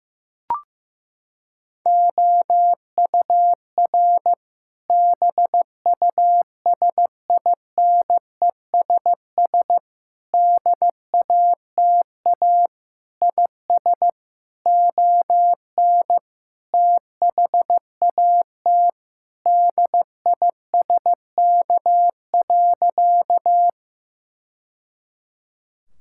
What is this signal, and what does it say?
Morse code, heard as a keyed tone, about 16 wpm, 700 Hz. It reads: OUR BUSINESS DATA IS ON THAT DISK.
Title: Sentences from Top 1000 Words - Review Part 6 - 15wpm